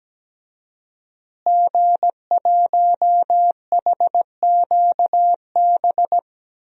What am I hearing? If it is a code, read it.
Morse code, heard as a keyed tone, about 17 wpm, 700 Hz. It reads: G1HQB
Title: DXCC-100 Call Signs - 5 Characters - Repeated 3x - 17wpm